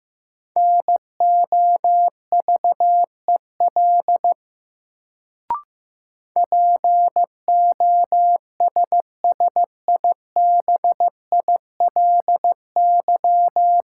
Morse code, heard as a keyed tone, about 15 wpm, 700 Hz. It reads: NOVEL POSSIBILY
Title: Sentences from Top 1300 Words - New Word Drill - 15wpm